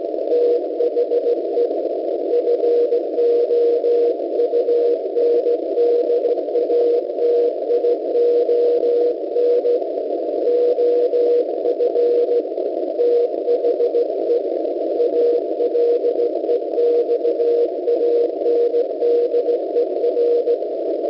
Morse code, heard as a keyed tone, about 17 wpm, 480 Hz, with noise. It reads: THE FOUNDATION OF THE ALEXANDER